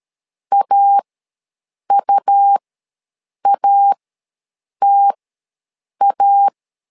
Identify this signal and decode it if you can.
Morse code, heard as a keyed tone, about 13 wpm, spaced as for 7 wpm, 800 Hz. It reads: AUATA
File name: LEZIONI / 2 UHDN / 2 U ws 12wpm 1655